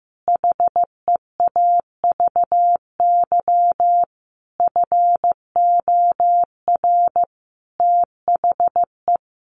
Morse code, heard as a keyed tone, about 15 wpm, 700 Hz. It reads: HEAVY FOR THE
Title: Sentences from Top 1100 Words 15wpm